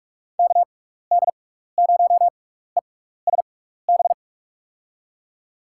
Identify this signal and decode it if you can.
Morse code, heard as a keyed tone, about 44 wpm, spaced as for 15 wpm, 700 Hz. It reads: KD0ESB
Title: USA Call Signs - 6 Characters - 6x Character Spacing - 45wpm